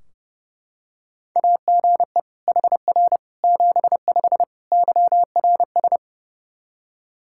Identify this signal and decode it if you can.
Morse code, heard as a keyed tone, about 29 wpm, 700 Hz. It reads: AGE HR 75 YRS
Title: K1USN SST QSO Practice 30wpm